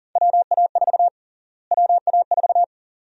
Morse code, heard as a keyed tone, about 40 wpm, 700 Hz. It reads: WA4 WA4
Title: USA Call Sign Prefixes - Repeated 2x - 40wpm